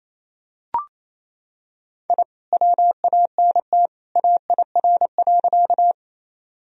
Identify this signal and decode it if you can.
Morse code, heard as a keyed tone, about 28 wpm, 700 Hz. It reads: I WANT AIR.
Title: Sentences from Top 1000 Words - Review Part 1 - 28wpm